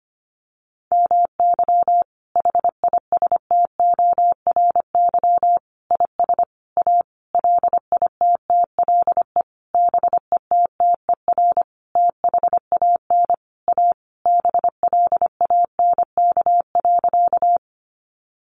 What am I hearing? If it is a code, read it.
Morse code, heard as a keyed tone, about 25 wpm, 700 Hz. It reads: MY HISTORY IS A LITTLE BETTER THAN A BLANK.
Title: Sentences from Top 1500 Words 25wpm